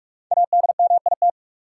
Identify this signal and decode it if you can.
Morse code, heard as a keyed tone, about 46 wpm, 700 Hz. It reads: ADMIT
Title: Five Letter Words - 45wpm